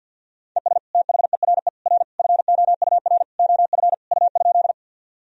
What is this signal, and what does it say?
Morse code, heard as a keyed tone, about 52 wpm, 700 Hz. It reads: ES THERE R FOUR OF U?
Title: Sentences from Top 300 Words and Abbreviations 50wpm